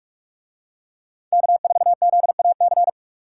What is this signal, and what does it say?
Morse code, heard as a keyed tone, about 44 wpm, 700 Hz. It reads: K4ZAC